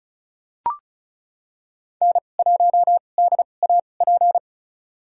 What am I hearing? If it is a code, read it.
Morse code, heard as a keyed tone, about 34 wpm, spaced as for 25 wpm, 700 Hz. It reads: N1DAP
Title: USA Call Signs - 5 Characters - 2x Character Spacing - 35wpm